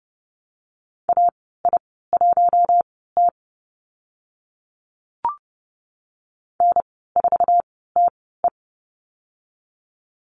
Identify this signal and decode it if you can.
Morse code, heard as a keyed tone, about 29 wpm, spaced as for 17 wpm, 700 Hz. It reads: AI1T N4TE
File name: Lesson 15g - Number 5 - Call Signs 30wpm